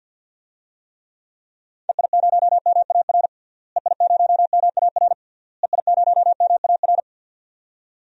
Morse code, heard as a keyed tone, about 46 wpm, 700 Hz. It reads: EI0MAR EI0MAR EI0MAR
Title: DXCC-100 Call Signs - 6 Characters - Repeated 3x - 50wpm